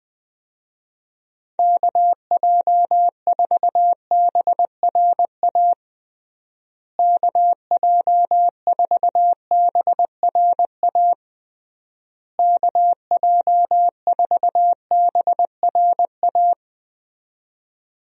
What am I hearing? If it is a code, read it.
Morse code, heard as a keyed tone, about 20 wpm, 700 Hz. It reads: KJ4BRA KJ4BRA KJ4BRA
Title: USA Call Signs - 6 Characters - Repeated 3x - 20wpm